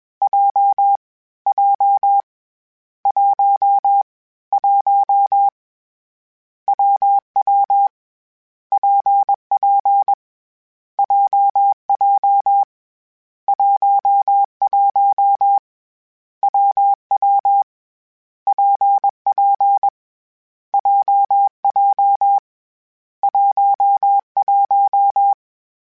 Morse code, heard as a keyed tone, about 21 wpm, 800 Hz. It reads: J J 1 1 WW PP JJ 11 WW PP JJ 11
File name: Difficult Combos / Difficult Combos-mp3s / WPJ WPJ1 @20